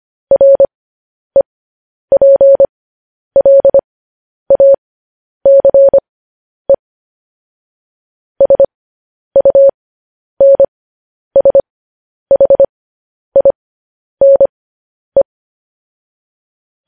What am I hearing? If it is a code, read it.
Morse code, heard as a keyed tone, about 25 wpm, spaced as for 10 wpm, 550 Hz. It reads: REPLACE SUNSHINE
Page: news headlines in morse code at 10 wpm